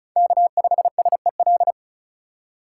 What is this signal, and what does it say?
Morse code, heard as a keyed tone, about 34 wpm, 700 Hz. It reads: K5SEL
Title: USA Call Signs - 2x Character Spacing - 35wpm